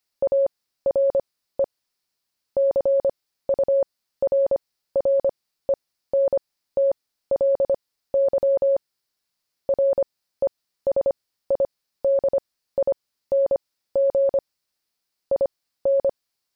Morse code, heard as a keyed tone, about 25 wpm, spaced as for 15 wpm, 550 Hz. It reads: ARE CURRENTLY RESIDING IN
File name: jb recaptured / morse